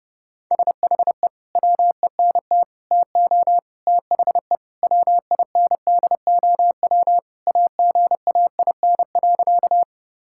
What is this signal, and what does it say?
Morse code, heard as a keyed tone, about 30 wpm, 700 Hz. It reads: SHE WENT TO THE WINDOW AGAIN.
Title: Sentences from Top 1600 Words 30wpm